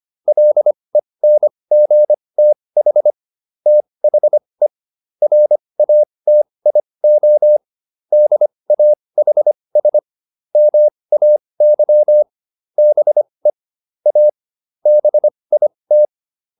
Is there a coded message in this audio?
Yes: Morse code from 0.3 to 16.1 s, heard as a keyed tone, about 25 wpm, spaced as for 20 wpm, 600 Hz. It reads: LENGTH THE RATIO DAHS MAY BE A BIT